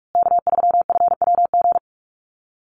Morse code, heard as a keyed tone, about 44 wpm, 700 Hz. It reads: K3FWG
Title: USA Call Signs Repeated 3x 45wpm